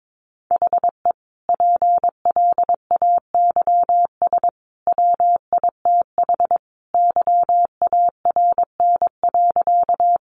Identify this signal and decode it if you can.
Morse code, heard as a keyed tone, about 22 wpm, 700 Hz. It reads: HE PLAYS WITH YARN.
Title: Easy Phrases 22wpm